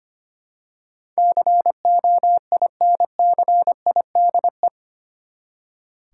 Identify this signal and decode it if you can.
Morse code, heard as a keyed tone, about 25 wpm, 700 Hz. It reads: COINCIDE
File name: Eight Letter Words - 25wpm